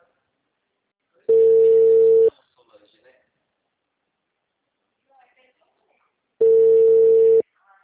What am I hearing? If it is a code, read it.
Ringback tone (China, 450 Hz), with speech.
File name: out- -2049- - - .69409